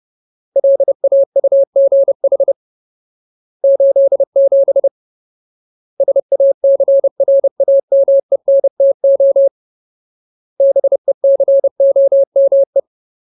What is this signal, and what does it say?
Morse code, heard as a keyed tone, about 30 wpm, 550 Hz. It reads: LAUGH 87 SACRAMENTO BECOME
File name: Ultimate Word List / 30WPM Ultimate